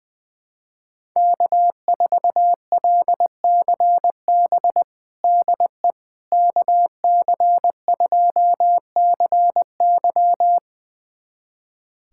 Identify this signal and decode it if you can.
Morse code, heard as a keyed tone, about 20 wpm, 700 Hz. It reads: K4LCB DE KC2CY
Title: USA Callsign DE Callsign - 5 Characters - 20wpm